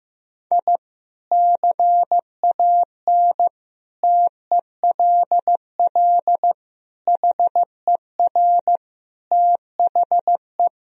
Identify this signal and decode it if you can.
Morse code, heard as a keyed tone, about 15 wpm, 700 Hz. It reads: I CAN TELL HER THE
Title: Sentences from Top 1000 Words - Review Part 3 - 15wpm